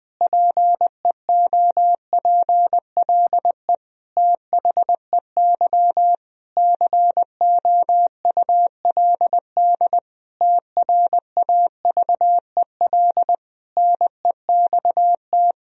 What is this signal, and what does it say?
Morse code, heard as a keyed tone, about 20 wpm, 700 Hz. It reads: PEOPLE THEY COULD TRAVEL NEXT
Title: Sentences from Top 1000 Words - Review Part 2 - 20wpm